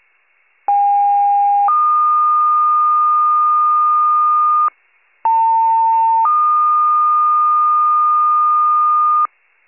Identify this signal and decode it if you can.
Two-tone page, heard as a one-second tone, then a three-second tone, about 810 then 1220 Hz, about 900 then 1220 Hz. It is heard from 0.7 to 9.3 s.
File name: pager / 73 RESCUE / wcalert